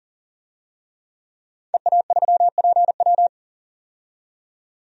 Morse code, heard as a keyed tone, about 40 wpm, 700 Hz. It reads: EA3PW
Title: DXCC-100 Call Signs - 5 Characters - 40wpm